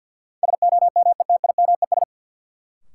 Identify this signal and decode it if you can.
Morse code, heard as a keyed tone, about 53 wpm, 700 Hz. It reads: SOMETIMES
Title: Sentences from Top 600 Words - New Word Drill - 50wpm